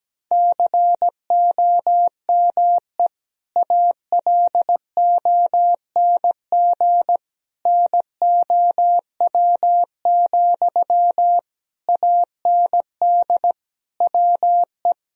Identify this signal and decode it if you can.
Morse code, heard as a keyed tone, about 17 wpm, 700 Hz. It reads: COME ALONG NOW, AND WE
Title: Sentences from Top 1000 Words - Review Part 4 - 17wpm